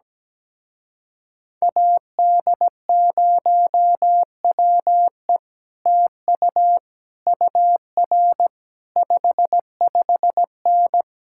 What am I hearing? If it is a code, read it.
Morse code, heard as a keyed tone, about 17 wpm, 700 Hz. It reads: AD0WE TU UR 55N
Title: Speed-Racing Common QSO Phrases Encore - 17wpm